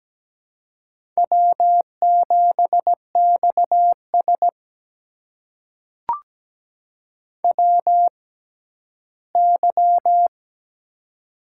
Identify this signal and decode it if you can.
Morse code, heard as a keyed tone, about 17 wpm, spaced as for 11 wpm, 700 Hz. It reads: W7XS WY
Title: USA Call Signs - 4 Characters - 6x Character Spacing - 17wpm